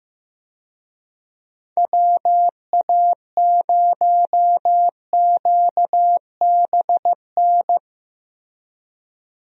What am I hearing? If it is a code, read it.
Morse code, heard as a keyed tone, about 15 wpm, 700 Hz. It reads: WA0QBN
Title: USA Call Signs - 6 Characters - Repeated 3x - 15wpm